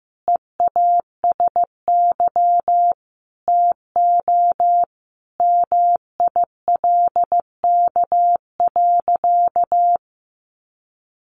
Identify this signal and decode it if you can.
Morse code, heard as a keyed tone, about 15 wpm, 700 Hz. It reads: EASY TO MILK.
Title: Sentences from Top 1500 Words 15wpm